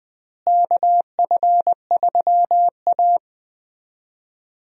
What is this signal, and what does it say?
Morse code, heard as a keyed tone, about 21 wpm, 700 Hz. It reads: KF3A
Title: USA Call Signs - 4 Characters - Repeated 2x - 20wpm